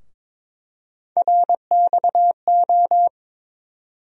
Morse code, heard as a keyed tone, about 22 wpm, 700 Hz. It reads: RXO
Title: USA Call Sign Postfixes - 3 Letters 22wpm